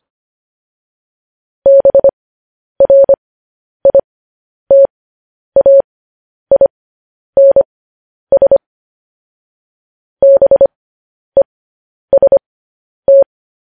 Morse code, heard as a keyed tone, about 25 wpm, spaced as for 10 wpm, 550 Hz. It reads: BRITAINS BEST
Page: news headlines in morse code at 10 wpm